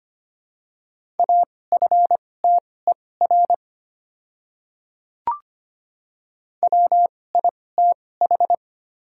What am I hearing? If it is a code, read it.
Morse code, heard as a keyed tone, about 25 wpm, spaced as for 18 wpm, 690 Hz. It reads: AFTER WITH
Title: 2x Character Spacing - Top 200 Words - 25wpm